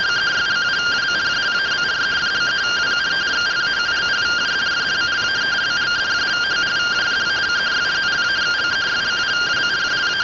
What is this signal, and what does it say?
Two-tone teleprinter signal, about 170 Hz shift.
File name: findings / 81 axial iodine 262.gen-27 rtty rx